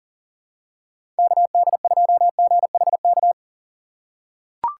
Morse code, heard as a keyed tone, about 40 wpm, 700 Hz. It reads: KD2GHK
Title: USA Call Signs - 6 Characters - 6x Character Spacing - 40wpm